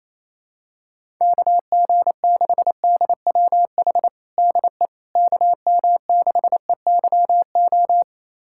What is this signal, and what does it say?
Morse code, heard as a keyed tone, about 28 wpm, 700 Hz. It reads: KG6DWH DE KM6EYO
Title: USA Callsign DE Callsign - 6 Characters - 28wpm